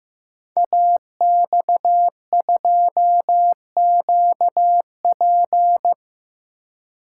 Morse code, heard as a keyed tone, about 15 wpm, 700 Hz. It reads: AX2QP
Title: Code Groups - 5 Letter:Numbers - 15wpm